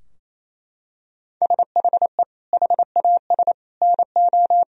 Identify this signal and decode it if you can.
Morse code, heard as a keyed tone, about 28 wpm, 700 Hz. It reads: SHE HAS NO
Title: Sentences from Top 1000 Words - Review Part 4 - 28wpm